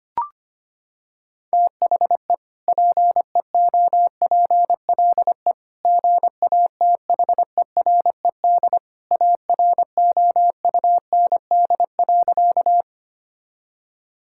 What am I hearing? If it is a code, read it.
Morse code, heard as a keyed tone, about 25 wpm, 700 Hz. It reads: THE PEOPLE GATHERED AROUND.
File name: practice / 25wpm / Sentences with 2-Word Phrases - 25wpm